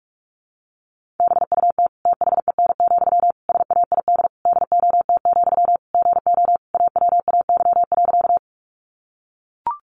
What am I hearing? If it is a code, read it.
Morse code, heard as a keyed tone, about 44 wpm, 700 Hz. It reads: BUT THEN, SAID DOT, GO AWAY.
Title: Sentences from Top 1500 Words 45wpm